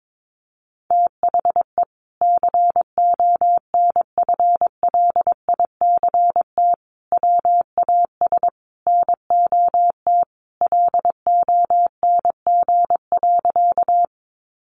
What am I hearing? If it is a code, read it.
Morse code, heard as a keyed tone, about 22 wpm, 700 Hz. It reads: THE CONFLICT WAS NOT LONG.
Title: Sentences from Top 1300 Words 22wpm